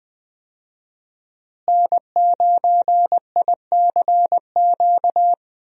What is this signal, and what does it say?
Morse code, heard as a keyed tone, about 20 wpm, 700 Hz. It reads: N9ICQ
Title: USA Call Signs - 5 Characters - Repeated 2x - 20wpm